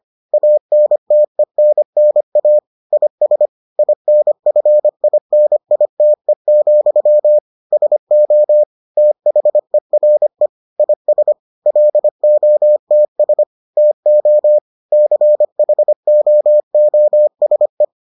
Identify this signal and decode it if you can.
Morse code, heard as a keyed tone, about 25 wpm, 600 Hz. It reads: ANTENNA IS INFINITE, SO THERE IS LOTS TO CHOOSE